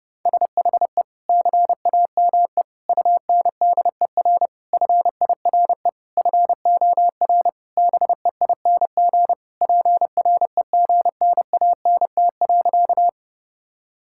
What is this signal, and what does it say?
Morse code, heard as a keyed tone, about 30 wpm, 700 Hz. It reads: SHE CAME UNDER FIRE FOR BEING PREGNANT.
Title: Sentences from Top 1000 Words - Review Part 6 - 30wpm